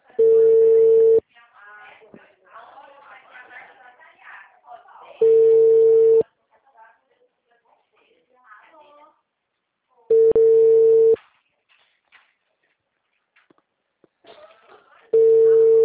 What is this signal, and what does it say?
Ringback tone (China, 450 Hz), with speech.